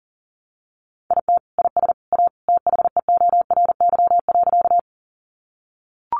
Morse code, heard as a keyed tone, about 40 wpm, 700 Hz. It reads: IT IS A THEORY.
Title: Sentences from Top 600 Words 40wpm